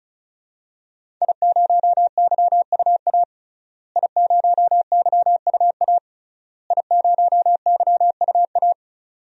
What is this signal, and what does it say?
Morse code, heard as a keyed tone, about 36 wpm, 700 Hz. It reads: I0YUA I0YUA I0YUA
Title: DXCC-100 Call Signs - 5 Characters - Repeated 3x - 35wpm